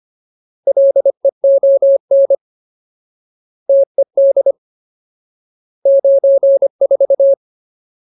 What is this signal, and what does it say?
Morse code, heard as a keyed tone, about 25 wpm, 550 Hz. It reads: LEON TED 94
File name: Ultimate Word List / 25WPM Ultimate